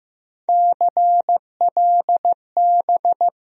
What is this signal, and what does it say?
Morse code, heard as a keyed tone, about 15 wpm, 700 Hz. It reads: CLB
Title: USA Call Sign Postfixes - 3 Letters 15wpm